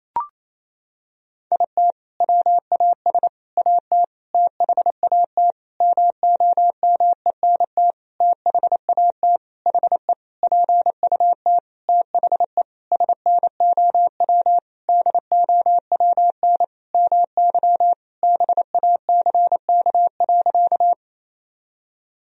Morse code, heard as a keyed tone, about 28 wpm, 700 Hz. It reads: IT WAS AT THAT MOMENT THAT HE PUT THE SNOW DOWN MY BACK.
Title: Sentences from Top 1000 Words - Review Part 3 - 28wpm